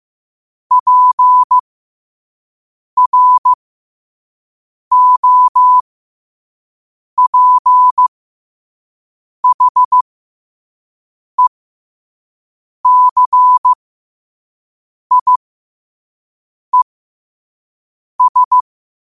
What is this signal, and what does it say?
Morse code, heard as a keyed tone, about 15 wpm, spaced as for 5 wpm, 1000 Hz. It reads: PROPHECIES